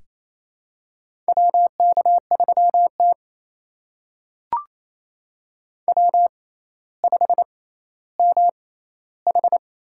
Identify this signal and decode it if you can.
Morse code, heard as a keyed tone, about 28 wpm, spaced as for 14 wpm, 700 Hz. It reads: WK3T W5MH
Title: USA Call Signs - 4 Characters - 6x Character Spacing - 28wpm